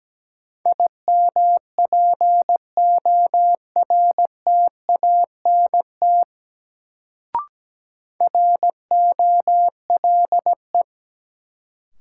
Morse code, heard as a keyed tone, about 17 wpm, 700 Hz. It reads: IMPORTANT ROLE
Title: Sets of 3 Words - New Word Drill - 17wpm